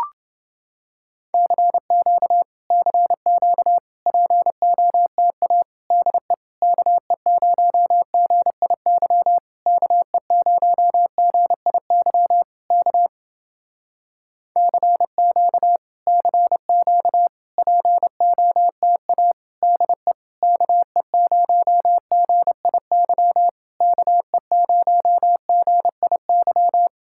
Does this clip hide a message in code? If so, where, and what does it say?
Yes: Morse code from 1.3 to 26.9 s, heard as a keyed tone, about 29 wpm, 700 Hz. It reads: CQ CQ POTA DE KE0GIY KE0GIY K CQ CQ POTA DE KE0GIY KE0GIY